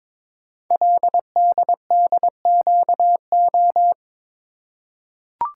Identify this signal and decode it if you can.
Morse code, heard as a keyed tone, about 22 wpm, 700 Hz. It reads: LDDQO
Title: Code Groups - 5 Letters - 22wpm